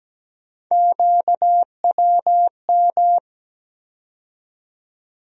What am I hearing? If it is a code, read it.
Morse code, heard as a keyed tone, about 17 wpm, 700 Hz. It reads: QWM
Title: USA Call Sign Postfixes - 3 Letters 17wpm